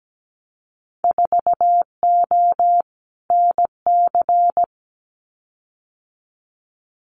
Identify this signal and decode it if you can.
Morse code, heard as a keyed tone, about 17 wpm, 700 Hz. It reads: 4O NC
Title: Winter Field Day Exchanges Encore 17wpm